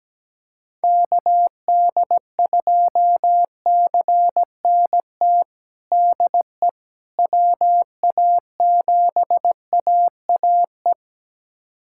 Morse code, heard as a keyed tone, about 17 wpm, 700 Hz. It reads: KD2CNT DE WA7AAE
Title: USA Callsign DE Callsign - 6 Characters - 17wpm